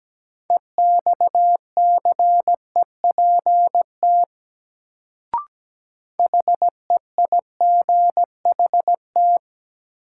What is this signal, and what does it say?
Morse code, heard as a keyed tone, about 17 wpm, 700 Hz. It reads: EXCEPT HEIGHT
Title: Sentences from Top 1600 Words - New Word Drill - 17wpm